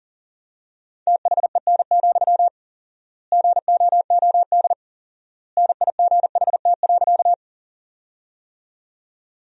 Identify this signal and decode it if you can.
Morse code, heard as a keyed tone, about 40 wpm, 700 Hz. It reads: THEN, GOOD NIGHT.
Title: 4x Word-Spacing - Sentences from Top 300 Words - 40wpm